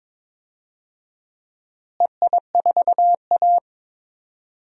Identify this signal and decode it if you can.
Morse code, heard as a keyed tone, about 22 wpm, 700 Hz. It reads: EI4A
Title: DXCC-100 Call Signs - 4 Characters - Repeated 3x - 22wpm